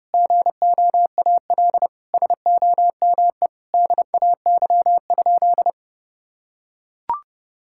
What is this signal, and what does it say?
Morse code, heard as a keyed tone, about 32 wpm, 700 Hz. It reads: GOAL SOME DAY?
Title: Sentences from Top 1000 Words - Review Part 6 - 30wpm